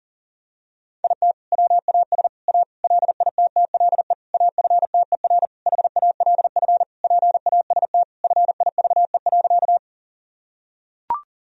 Morse code, heard as a keyed tone, about 40 wpm, 700 Hz. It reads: IT WAS A LITTLE AFTER HALF PAST FIVE.
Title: Sentences from Top 1000 Words - Review Part 3 - 40wpm